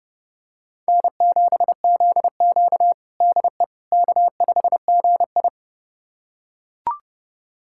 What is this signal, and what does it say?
Morse code, heard as a keyed tone, about 30 wpm, 700 Hz. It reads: N7ZQ DE K5GI
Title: USA Callsign DE Callsign - 4 Characters - 30wpm